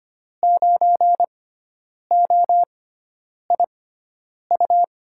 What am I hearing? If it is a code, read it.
Morse code, heard as a keyed tone, about 25 wpm, spaced as for 9 wpm, 700 Hz. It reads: 9OIU
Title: USA Call Signs - 6 Characters - 6x Character Spacing - 25wpm